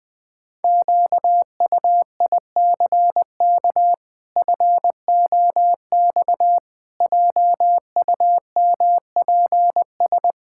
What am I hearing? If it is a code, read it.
Morse code, heard as a keyed tone, about 20 wpm, 700 Hz. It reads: QUICK FOX JUMPS
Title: Pangram Challenge v2 - 20wpm